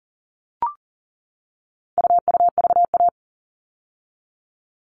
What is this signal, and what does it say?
Morse code, heard as a keyed tone, about 40 wpm, 700 Hz. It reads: UUVA
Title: All Cut Numbers - 4-Digit 40wpm